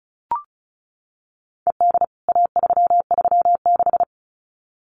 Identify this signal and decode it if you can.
Morse code, heard as a keyed tone, about 34 wpm, 700 Hz. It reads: ED A336
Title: CWT Exchange 35wpm